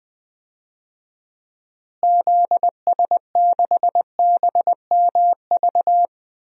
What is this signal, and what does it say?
Morse code, heard as a keyed tone, about 21 wpm, 700 Hz. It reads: ZS6BMV
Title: DXCC-100 Call Signs - 6 Characters - Repeated 3x - 20wpm